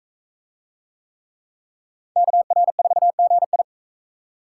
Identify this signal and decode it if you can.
Morse code, heard as a keyed tone, about 38 wpm, 700 Hz. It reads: KR4GI